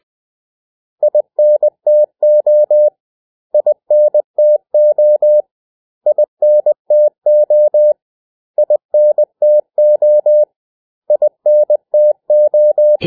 Morse code, heard as a keyed tone, about 20 wpm, 600 Hz. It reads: INTO INTO INTO INTO INTO
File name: cw / 100 Common English Words